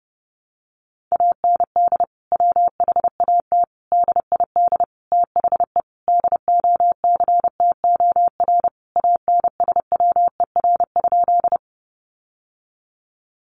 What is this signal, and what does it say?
Morse code, heard as a keyed tone, about 30 wpm, 700 Hz. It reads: AND WHAT DID THE DOCTOR ANSWER?
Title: Sentences from Top 1500 Words 30wpm